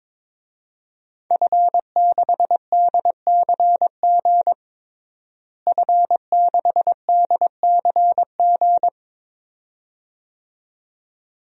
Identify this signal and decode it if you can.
Morse code, heard as a keyed tone, about 22 wpm, 700 Hz. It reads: F6DCG F6DCG
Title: DXCC-100 Call Signs - 5 Characters - Repeated 2x - 22wpm